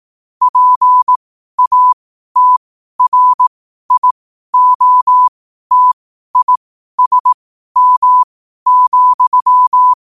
Morse code, heard as a keyed tone, about 18 wpm, spaced as for 12 wpm, 1000 Hz. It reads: PATRIOTISM,
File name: qotdfiles / 12wpm